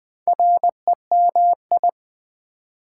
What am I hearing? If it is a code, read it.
Morse code, heard as a keyed tone, about 20 wpm, 700 Hz. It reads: REMI
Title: Build A Word - 4 Letters At A Time - 8-Letter Words - 20wpm